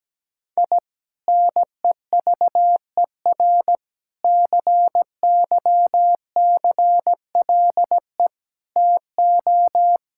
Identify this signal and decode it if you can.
Morse code, heard as a keyed tone, about 17 wpm, 700 Hz. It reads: I NEVER CYCLE TO
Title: Sentences from Top 1000 Words - Review Part 6 - 17wpm